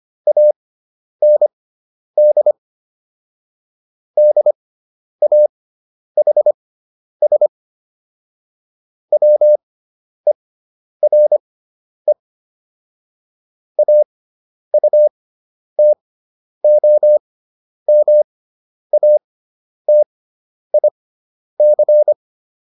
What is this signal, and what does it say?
Morse code, heard as a keyed tone, about 25 wpm, spaced as for 10 wpm, 600 Hz. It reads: AND DAHS WERE AUTOMATIC